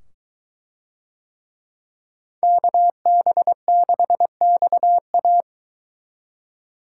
Morse code, heard as a keyed tone, about 21 wpm, 700 Hz. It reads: KB6XA